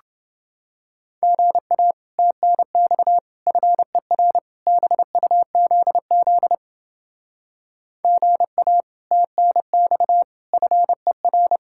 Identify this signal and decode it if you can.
Morse code, heard as a keyed tone, about 30 wpm, 700 Hz. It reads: GA TNX FER BUZZ GA TNX FER